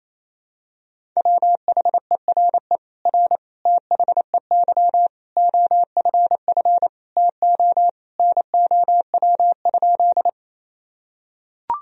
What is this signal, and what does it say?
Morse code, heard as a keyed tone, about 28 wpm, 700 Hz. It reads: WHERE R THEY OFF TO NOW?
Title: Sentences from Top 200 Words and Abbreviations 28wpm